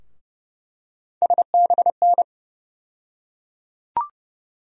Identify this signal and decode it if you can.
Morse code, heard as a keyed tone, about 32 wpm, 700 Hz. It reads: SBN
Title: USA Call Sign Postfixes - 3 Letters 30wpm